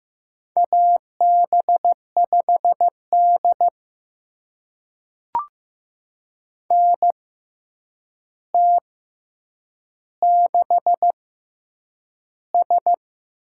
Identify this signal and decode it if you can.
Morse code, heard as a keyed tone, about 15 wpm, 700 Hz. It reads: A B 5 D N T 6 S
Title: USA Call Signs - 4 Characters - 6x Character Spacing - 15wpm